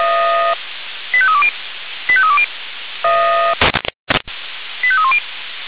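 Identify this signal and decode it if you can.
Five-tone selective call (ZVEI), heard as quick runs of five tones, tones of about 70 ms. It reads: 85320 85320 85310
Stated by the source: ZVEI five-tone call 85312 received by SDR.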